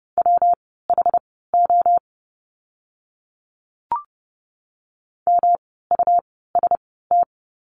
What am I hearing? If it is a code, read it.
Morse code, heard as a keyed tone, about 30 wpm, spaced as for 17 wpm, 700 Hz. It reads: WHO MUST
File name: course / 30:10wpm / 237 Lesson 38f - Letter q - Words 30wpm